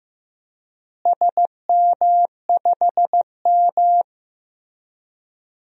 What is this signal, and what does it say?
Morse code, heard as a keyed tone, about 15 wpm, 700 Hz. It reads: SM5M
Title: DXCC-100 Call Signs - 4 Characters - Repeated 3x - 15wpm